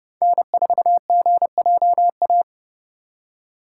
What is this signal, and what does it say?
Morse code, heard as a keyed tone, about 30 wpm, 700 Hz. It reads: N4GJA
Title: USA Call Signs - 5 Characters - 4x Character Spacing - 30wpm